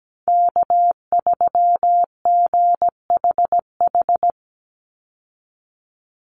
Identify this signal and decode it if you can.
Morse code, heard as a keyed tone, about 17 wpm, 700 Hz. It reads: K3GHH
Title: Call Signs - 17wpm